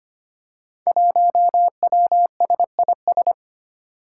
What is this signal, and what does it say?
Morse code, heard as a keyed tone, about 25 wpm, 700 Hz. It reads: 1WSIS
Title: Five Challenging Letter-Number Combinations 25wpm